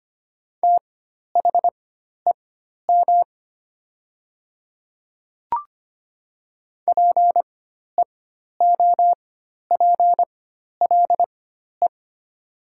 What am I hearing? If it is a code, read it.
Morse code, heard as a keyed tone, about 25 wpm, spaced as for 12 wpm, 700 Hz. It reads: THEM PEOPLE
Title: 4x Character Spacing - Top 300 Words - 25wpm